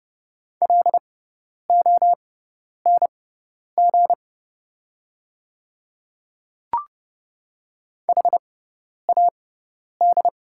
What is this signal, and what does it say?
Morse code, heard as a keyed tone, about 31 wpm, spaced as for 10 wpm, 700 Hz. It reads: LONG HAD
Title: 6x Character Spacing - Top 100 Words - 30wpm